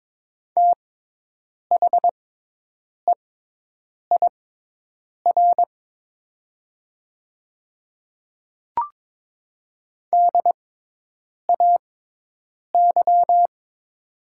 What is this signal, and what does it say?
Morse code, heard as a keyed tone, about 22 wpm, spaced as for 8 wpm, 700 Hz. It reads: THEIR DAY
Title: 6x Character Spacing - Top 100 Words - 22wpm